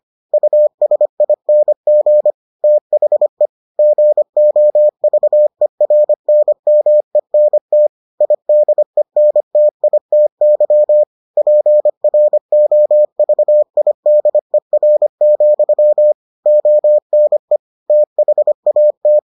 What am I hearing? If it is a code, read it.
Morse code, heard as a keyed tone, about 25 wpm, 600 Hz. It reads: USING THE GOVERNMENT IDENTITY PROVIDER, ONE THAT